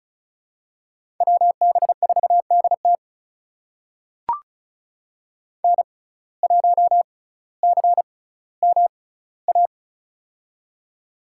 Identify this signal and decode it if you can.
Morse code, heard as a keyed tone, about 34 wpm, spaced as for 18 wpm, 700 Hz. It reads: WB4DT N1CMA
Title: USA Call Signs - 5 Characters - 6x Character Spacing - 35wpm